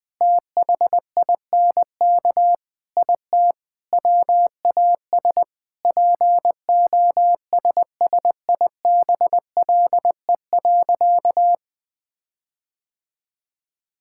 Morse code, heard as a keyed tone, about 20 wpm, 700 Hz. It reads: THINK IT WAS POSSIBLE.
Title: Sentences with 4-Word Phrases - 20wpm